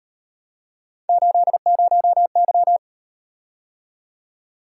Morse code, heard as a keyed tone, about 38 wpm, 700 Hz. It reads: 80Y